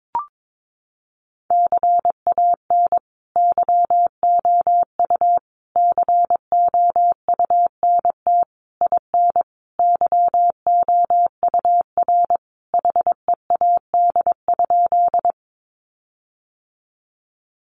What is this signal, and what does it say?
Morse code, heard as a keyed tone, about 22 wpm, 700 Hz. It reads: CAN YOU COUNT IN YOUR HEAD?
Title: Sentences from Top 1500 Words 22wpm